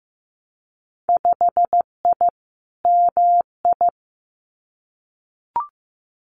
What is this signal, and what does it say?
Morse code, heard as a keyed tone, about 15 wpm, 700 Hz. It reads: 5I MI
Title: Winter Field Day Exchanges Encore 15wpm